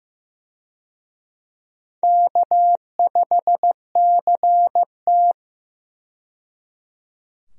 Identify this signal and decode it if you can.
Morse code, heard as a keyed tone, about 15 wpm, 700 Hz. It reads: K5CT
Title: USA Call Signs - 4 Characters - 15wpm